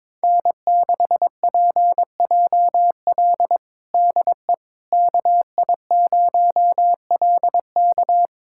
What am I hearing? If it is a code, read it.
Morse code, heard as a keyed tone, about 21 wpm, 700 Hz. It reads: N6PJL DE KI0LK
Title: USA Callsign DE Callsign - 5 Characters - 22wpm